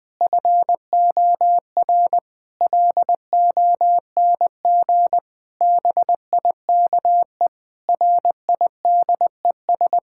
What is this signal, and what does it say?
Morse code, heard as a keyed tone, about 20 wpm, 700 Hz. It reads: FOR LONG BIKE RIDES
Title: Difficult Sentences with Hints - 20wpm